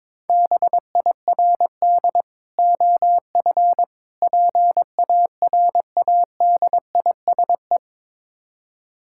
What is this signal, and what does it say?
Morse code, heard as a keyed tone, about 22 wpm, 690 Hz. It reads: BIRD OF PARADISE
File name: Difficult Words with Hints - 22wpm